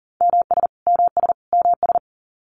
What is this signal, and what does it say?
Morse code, heard as a keyed tone, about 40 wpm, 700 Hz. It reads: MS MS MS
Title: US States Repeated 3x 40wpm